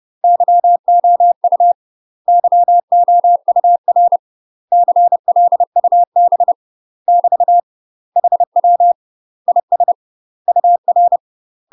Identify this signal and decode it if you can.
Morse code, heard as a keyed tone, about 30 wpm, 700 Hz. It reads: YOU YOUR CLUB <BT> HW IS UR
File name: UK MASTER QSOs TRAINING / 30WPM / UK MASTER QSOs 0011